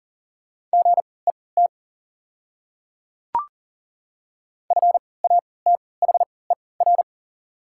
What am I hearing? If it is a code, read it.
Morse code, heard as a keyed tone, about 40 wpm, spaced as for 23 wpm, 700 Hz. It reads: GET FATHER
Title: 3x Character Spacing - Top 200 Words - 40wpm